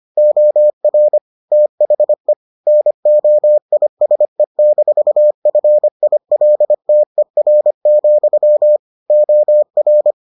Morse code, heard as a keyed tone, about 25 wpm, 600 Hz. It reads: OR THE NOISE-FILTER, OR